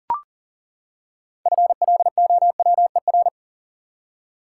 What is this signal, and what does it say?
Morse code, heard as a keyed tone, about 40 wpm, 700 Hz. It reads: FLOWER
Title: Six Letter Words - 40wpm